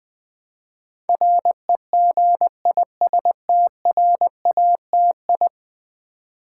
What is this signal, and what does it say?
Morse code, heard as a keyed tone, about 20 wpm, 700 Hz. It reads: REGISTRATI
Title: Build A Word - 5 Letters At A Time - 12-Letter Words - 20wpm